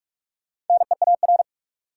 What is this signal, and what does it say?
Morse code, heard as a keyed tone, about 42 wpm, 700 Hz. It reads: NEAR
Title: Sentences from Top 200 Words - New Word Drill - 45wpm